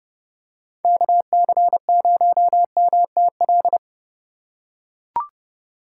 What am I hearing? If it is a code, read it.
Morse code, heard as a keyed tone, about 31 wpm, 700 Hz. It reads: KC0MTL
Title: USA Call Signs - 3x Character Spacing - 30wpm